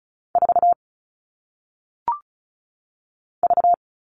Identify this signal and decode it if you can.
Morse code, heard as a keyed tone, about 36 wpm, 700 Hz. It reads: 4 V
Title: All Cut Numbers - 1-Digit 35wpm